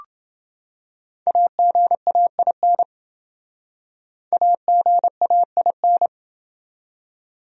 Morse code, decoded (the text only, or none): AGAIN AGAIN